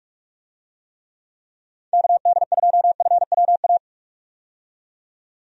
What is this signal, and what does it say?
Morse code, heard as a keyed tone, about 46 wpm, 700 Hz. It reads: KD2FWA
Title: USA Call Signs - 6 Characters - 45wpm